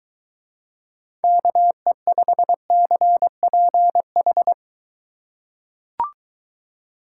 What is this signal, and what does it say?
Morse code, heard as a keyed tone, about 22 wpm, 700 Hz. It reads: KE5CPH